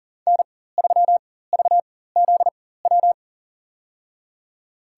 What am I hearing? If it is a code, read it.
Morse code, heard as a keyed tone, about 40 wpm, spaced as for 19 wpm, 700 Hz. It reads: N3VZW
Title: USA Call Signs - 5 Characters - 4x Character Spacing - 40wpm